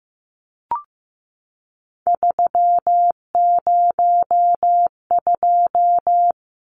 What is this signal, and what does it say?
Morse code, heard as a keyed tone, about 15 wpm, 700 Hz. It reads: 302
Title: Numbers - 3-Digit 15wpm